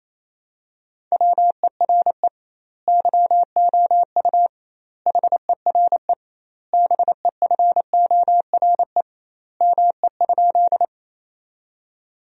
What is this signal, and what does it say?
Morse code, heard as a keyed tone, about 27 wpm, 700 Hz. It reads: WERE YOU HERE BEFORE ME?
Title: 2x Word-Spacing - Sentences from Top 200 Words - 28wpm